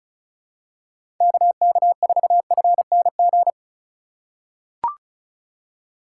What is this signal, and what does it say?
Morse code, heard as a keyed tone, about 36 wpm, 700 Hz. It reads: KK4FNG
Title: USA Call Signs - 6 Characters - 8x Character Spacing - 35wpm